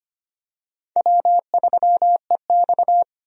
Morse code, heard as a keyed tone, about 25 wpm, 700 Hz. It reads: W3EX
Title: USA Call Signs - 4 Characters - 25wpm